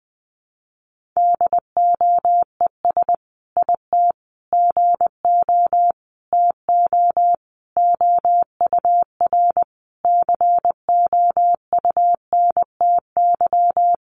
Morse code, heard as a keyed tone, about 20 wpm, 700 Hz. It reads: DOES IT GO TO OUR COUNTY
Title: Sentences from Top 1500 Words 20wpm